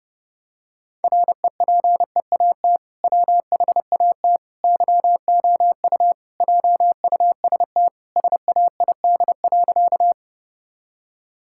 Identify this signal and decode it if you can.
Morse code, heard as a keyed tone, about 30 wpm, 700 Hz. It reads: REPEAT WHAT YOU JUST SAID.
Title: Sentences from Top 1600 Words 30wpm